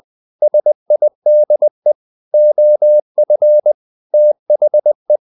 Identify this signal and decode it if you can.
Morse code, heard as a keyed tone, about 20 wpm, 600 Hz. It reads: SIDE OF THE
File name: the old man and the sea / the old man and the sea part 1 20wpm